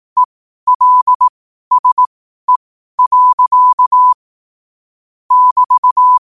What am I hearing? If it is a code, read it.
Morse code, heard as a keyed tone, about 18 wpm, spaced as for 12 wpm, 1000 Hz. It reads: ELSE. <BT>
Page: qotd-at-12-wpm